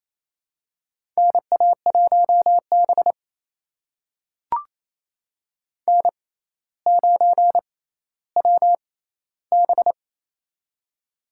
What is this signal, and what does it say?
Morse code, heard as a keyed tone, about 28 wpm, spaced as for 14 wpm, 700 Hz. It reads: NA1B N9WB
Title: USA Call Signs - 4 Characters - 6x Character Spacing - 28wpm